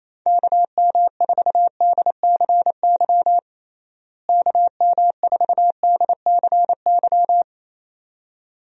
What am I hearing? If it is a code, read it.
Morse code, heard as a keyed tone, about 28 wpm, 700 Hz. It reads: KM4DCY KM4DCY
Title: USA Call Signs - 6 Characters - Repeated 2x - 28wpm